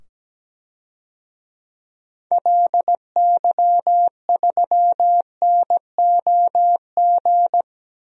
Morse code, heard as a keyed tone, about 17 wpm, 700 Hz. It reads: LY3NOG
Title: DXCC-100 Call Signs - 6 Characters - 17wpm